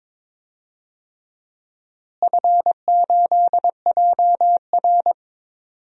Morse code, heard as a keyed tone, about 22 wpm, 700 Hz. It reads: F8JR
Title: DXCC-100 Call Signs - 4 Characters - Repeated 3x - 22wpm